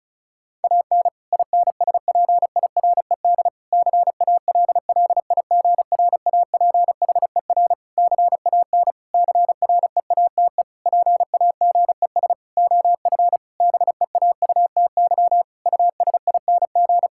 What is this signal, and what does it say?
Morse code, heard as a keyed tone, about 34 wpm, 700 Hz. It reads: AN INSPIRED CALLIGRAPHER CAN CREATE PAGES OF BEAUTY USING
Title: Pangram Challenge v2 - 35wpm